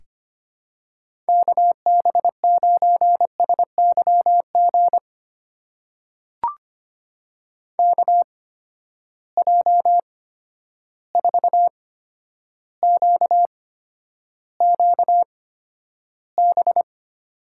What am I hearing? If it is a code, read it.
Morse code, heard as a keyed tone, about 25 wpm, spaced as for 11 wpm, 700 Hz. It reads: KB9SYG KJ4QQB
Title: USA Call Signs - 6 Characters - 8x Character Spacing - 25wpm